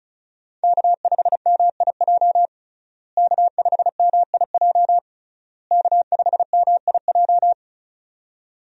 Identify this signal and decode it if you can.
Morse code, heard as a keyed tone, about 34 wpm, 700 Hz. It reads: K5MIJ K5MIJ K5MIJ
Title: USA Call Signs - 5 Characters - Repeated 3x - 35wpm